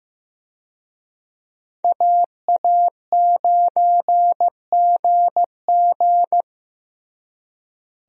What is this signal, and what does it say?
Morse code, heard as a keyed tone, about 15 wpm, 700 Hz. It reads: AA9GG
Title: USA Call Signs - 5 Characters - Repeated 2x - 15wpm